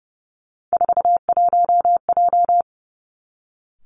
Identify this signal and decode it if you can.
Morse code, heard as a keyed tone, about 30 wpm, 700 Hz. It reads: 41J
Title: Three Challenging Letter-Number Combinations 30wpm